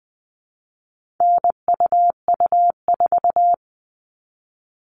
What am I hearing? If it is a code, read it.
Morse code, heard as a keyed tone, about 20 wpm, 700 Hz. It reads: NUU4
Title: All Cut Numbers - 4-Digit 20wpm